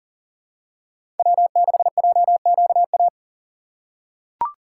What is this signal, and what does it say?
Morse code, heard as a keyed tone, about 40 wpm, 700 Hz. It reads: W6JQA
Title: USA Call Signs - 5 Characters - 6x Character Spacing - 40wpm